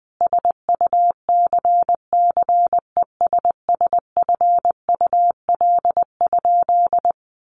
Morse code, heard as a keyed tone, about 19 wpm, 700 Hz. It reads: SUCCESSFUL?
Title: A Sound Of Thunder 20wpm